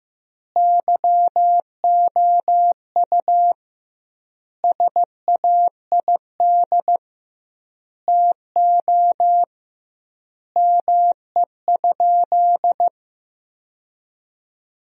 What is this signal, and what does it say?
Morse code, heard as a keyed tone, about 16 wpm, 700 Hz. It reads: YOU SAID TO ME?
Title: 2x Word-Spacing - Sentences from Top 200 Words - 15wpm